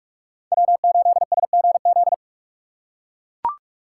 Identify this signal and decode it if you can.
Morse code, heard as a keyed tone, about 43 wpm, 700 Hz. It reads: W8SGZ
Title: USA Call Signs - 5 Characters - 6x Character Spacing - 45wpm